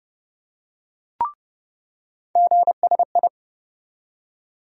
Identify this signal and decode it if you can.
Morse code, heard as a keyed tone, about 30 wpm, 700 Hz. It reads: GSI
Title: USA Call Sign Postfixes - 3 Letters 30wpm